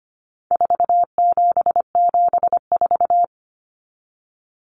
Morse code, heard as a keyed tone, about 25 wpm, 700 Hz. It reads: 4774
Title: Common Cut Numbers - 4-Digit 25wpm